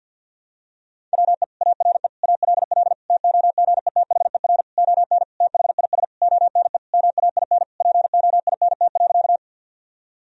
Morse code, heard as a keyed tone, about 52 wpm, 700 Hz. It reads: WE ARE ALL TOGETHER ON THIS ONE MAIN POINT.
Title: Sentences from Top 1000 Words - Review Part 2 - 50wpm